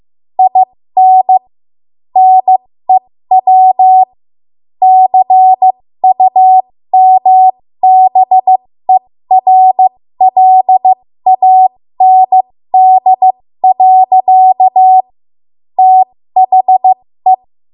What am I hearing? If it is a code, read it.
Morse code, heard as a keyed tone, about 15 wpm, spaced as for 13 wpm, 750 Hz. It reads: IN NEW CUMBERLAND. THE
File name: Morse / 13WPM